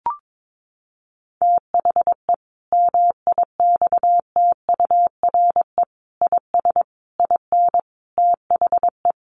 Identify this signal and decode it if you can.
Morse code, heard as a keyed tone, about 22 wpm, 700 Hz. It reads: THE MIXTURE IS IN THE
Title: Sentences from Top 1500 Words 22wpm